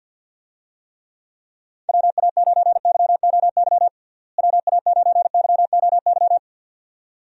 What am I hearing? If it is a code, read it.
Morse code, heard as a keyed tone, about 47 wpm, 700 Hz. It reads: WA9YOY WA9YOY
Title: USA Call Signs - 6 Characters - Repeated 2x - 50wpm